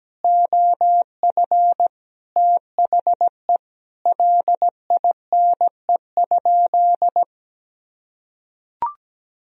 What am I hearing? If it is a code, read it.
Morse code, heard as a keyed tone, about 17 wpm, 700 Hz. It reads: OF THE LINE?
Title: Sentences from Top 1000 Words - Review Part 1 - 17wpm